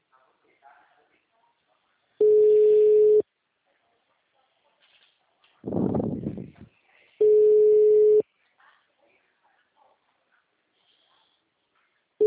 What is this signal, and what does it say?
Ringback tone (Europe, 425 Hz), with speech.
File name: out- -2020- - - .1122